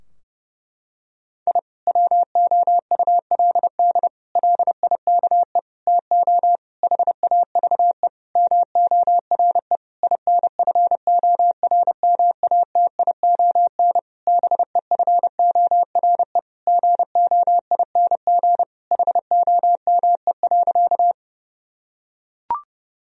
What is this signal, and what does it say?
Morse code, heard as a keyed tone, about 31 wpm, 700 Hz. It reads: I WOULD LIKE TO HAVE MORE INFORMATION BEFORE GOING HOME.
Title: Sentences from Top 1000 Words - Review Part 3 - 30wpm